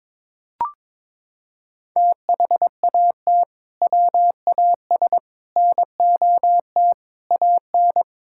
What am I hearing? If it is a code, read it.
Morse code, heard as a keyed tone, about 22 wpm, 700 Hz. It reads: THAT WAS NOT AN
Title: Sentences from Top 1000 Words - Review Part 5 - 22wpm